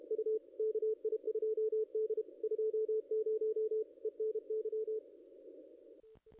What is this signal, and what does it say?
Morse code, heard as a keyed tone, about 31 wpm, 440 Hz, with radio noise. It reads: U KI2D 20 ENY
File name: contests / ss-cw-k5tr-op-n5rz -0200